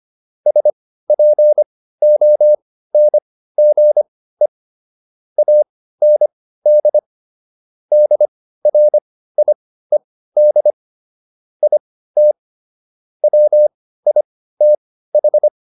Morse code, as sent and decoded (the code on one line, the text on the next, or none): ... .--. --- -. --. / .- -. -.. / -.. .-. .. -.. / .. - / .-- .. - ....
SPONGE AND DRIED IT WITH